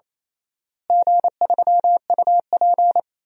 Morse code, heard as a keyed tone, about 28 wpm, 700 Hz. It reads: G3UP
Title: DXCC-100 Call Signs - 4 Characters - Repeated 2x - 28wpm